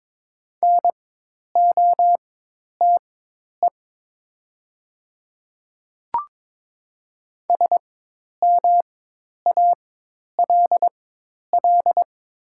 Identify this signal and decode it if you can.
Morse code, heard as a keyed tone, about 22 wpm, spaced as for 10 wpm, 700 Hz. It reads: NOTE SMALL
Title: 4x Character Spacing - Top 500 Words - 22wpm